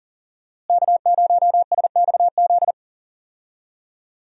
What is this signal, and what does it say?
Morse code, heard as a keyed tone, about 40 wpm, 700 Hz. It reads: K0SXZ
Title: USA Call Signs - 5 Characters - Repeated 3x - 40wpm